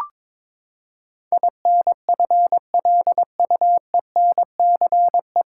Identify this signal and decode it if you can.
Morse code, heard as a keyed tone, about 22 wpm, 700 Hz. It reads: INFLUENCE
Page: Morse Code Ninja practice 22wpm